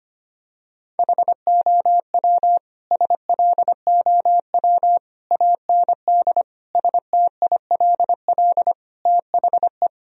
Morse code, heard as a keyed tone, about 25 wpm, 700 Hz. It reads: HOW SLOW AND STILL THE
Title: Sentences from Top 1000 Words - Review Part 2 - 25wpm